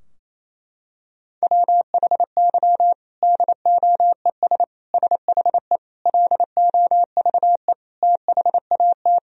Morse code, heard as a keyed tone, about 28 wpm, 700 Hz. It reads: WHY DOES SHE LOVE THAT
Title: Sentences from Top 1000 Words - Review Part 4 - 28wpm